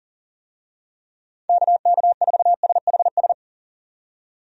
Morse code, heard as a keyed tone, about 40 wpm, 700 Hz. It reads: KK4SHS
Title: USA Call Signs - 6 Characters - 40wpm